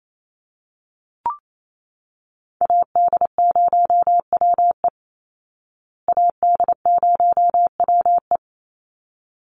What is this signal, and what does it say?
Morse code, heard as a keyed tone, about 28 wpm, 700 Hz. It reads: AD0WE AD0WE
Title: Call Signs Repeated 3x - 28wpm